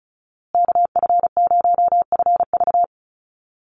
Morse code, heard as a keyed tone, about 34 wpm, 700 Hz. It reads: KF0FV
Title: USA Call Signs Repeated 3x 35wpm